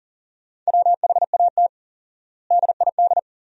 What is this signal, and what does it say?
Morse code, heard as a keyed tone, about 40 wpm, 700 Hz. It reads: WHAT DID
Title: 4x Word-Spacing - Sentences from Top 300 Words - 40wpm